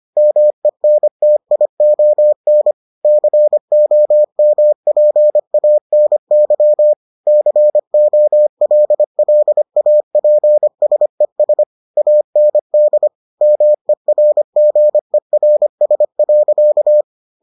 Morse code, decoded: MENTION COMPANY COLLAPSES AND MERGERS.

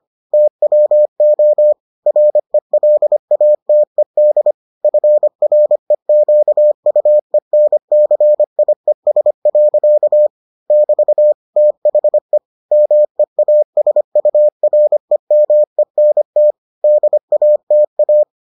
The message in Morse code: TWO RELATED FREQUENCIES. <BT> THE MEASUREMENT DATA

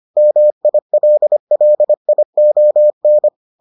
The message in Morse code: MILLION

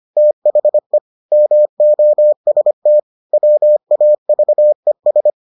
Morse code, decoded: THE MOST WAVES

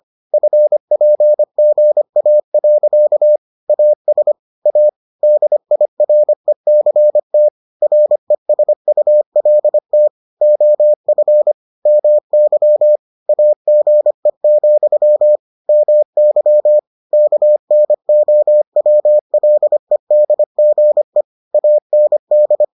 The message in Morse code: FPGA. AS A DIRECT RESULT OF MY AGE, MY KNOWLEDGE AND

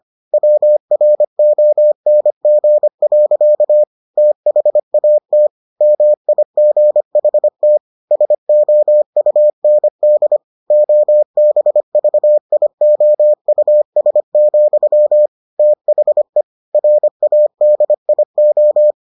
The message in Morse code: WRONG. THAT MIGHT SOUND OBVIOUS, THE RADIO

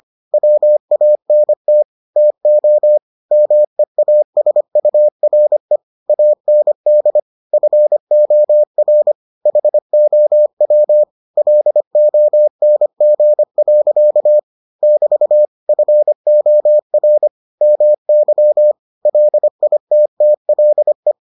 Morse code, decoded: WANT TO MEASURE AND FOR HOW LONG. <BT> FOR MY LITTLE

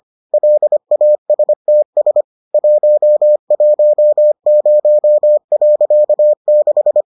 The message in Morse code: LASTS 110.6

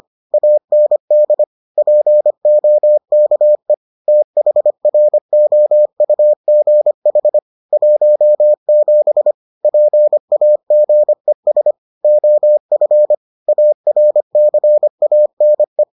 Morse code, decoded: AND POKE THROUGH 17 PAGES OF ARCANE

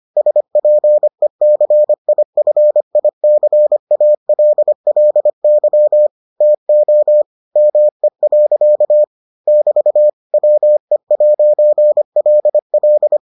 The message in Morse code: SPECIFICALLY TO ME. <BT> WE'LL

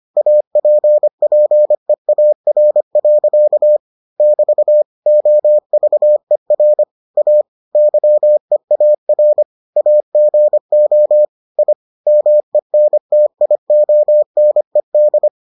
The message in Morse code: APPEAR. <BT> OVER A YEAR AGO I MENTIONED

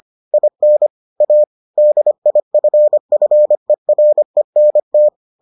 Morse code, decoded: IN A DIFFERENT

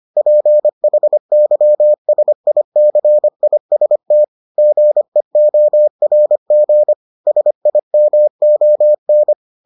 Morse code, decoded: PHYSICIST GEORG SIMON

